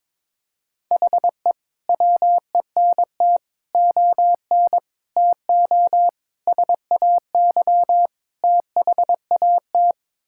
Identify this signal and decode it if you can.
Morse code, heard as a keyed tone, about 22 wpm, 700 Hz. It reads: HE WENT ON TO SAY THAT
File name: practice / 22wpm / Sets of 6 Words - 22wpm